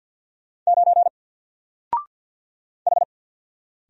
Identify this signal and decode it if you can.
Morse code, heard as a keyed tone, about 48 wpm, 700 Hz. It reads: 9 H
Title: Single Letter-Number Alt 50wpm